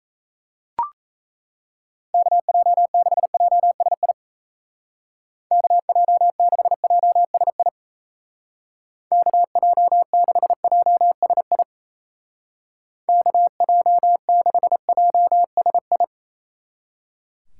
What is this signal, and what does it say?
Morse code, heard as a keyed tone, about 37 wpm, 700 Hz. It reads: KJ6JSI KJ6JSI KJ6JSI KJ6JSI